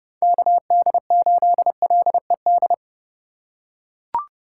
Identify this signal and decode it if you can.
Morse code, heard as a keyed tone, about 27 wpm, 700 Hz. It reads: KD8LED